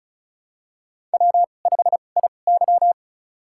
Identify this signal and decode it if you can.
Morse code, heard as a keyed tone, about 34 wpm, spaced as for 25 wpm, 700 Hz. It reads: W5IY
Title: USA Call Signs - 4 Characters - 2x Character Spacing - 35wpm